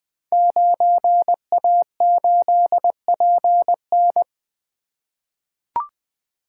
Morse code, heard as a keyed tone, about 20 wpm, 700 Hz. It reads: 9A8PN
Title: Code Groups - 5 Letter:Numbers - 20wpm